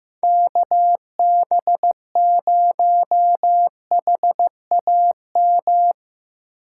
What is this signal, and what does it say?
Morse code, heard as a keyed tone, about 17 wpm, 700 Hz. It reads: KB0HAM
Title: Speed Racing - USA 2x3 Call Signs 15wpm